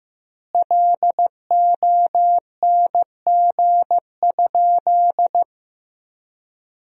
Morse code, heard as a keyed tone, about 15 wpm, 700 Hz. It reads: LONG?
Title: 4x Word-Spacing - Sentences from Top 500 Words - 15wpm